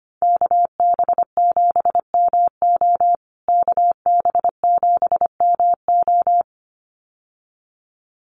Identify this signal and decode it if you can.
Morse code, heard as a keyed tone, about 25 wpm, 700 Hz. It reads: KB7MO KB7MO
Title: USA Call Signs Repeated 2x 25wpm